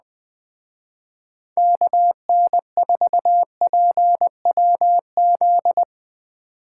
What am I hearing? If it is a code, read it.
Morse code, heard as a keyed tone, about 20 wpm, 700 Hz. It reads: KN4PWZ
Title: Speed Racing - USA Call Signs 17wpm